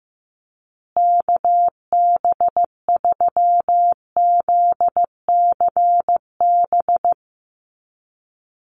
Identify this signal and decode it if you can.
Morse code, heard as a keyed tone, about 14 wpm, 700 Hz. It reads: KB3ZCB
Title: USA Call Signs Repeated 2x 15wpm